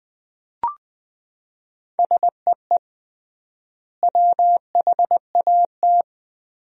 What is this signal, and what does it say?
Morse code, heard as a keyed tone, about 20 wpm, 700 Hz. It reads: SEE WHAT
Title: 3x Word-Spacing - Sentences from Top 500 Words - 20wpm